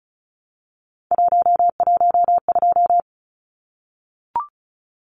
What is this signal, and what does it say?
Morse code, heard as a keyed tone, about 34 wpm, 700 Hz. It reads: 112
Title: Numbers - 3-Digit 35wpm